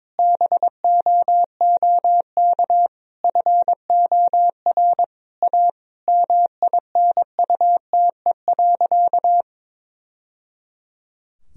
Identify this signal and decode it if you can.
Morse code, heard as a keyed tone, about 22 wpm, 700 Hz. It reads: BOOK FOR A MINUTE.
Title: Sentences from Top 1000 Words - Review Part 2 - 22wpm